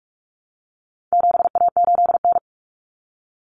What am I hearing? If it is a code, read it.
Morse code, heard as a keyed tone, about 44 wpm, 700 Hz. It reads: 7A8N
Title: Common Cut Numbers - 4-Digit 45wpm